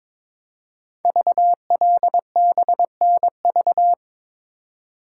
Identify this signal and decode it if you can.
Morse code, heard as a keyed tone, about 22 wpm, 700 Hz. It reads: VLBNV
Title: Code Groups - 5 Letter:Numbers - 22wpm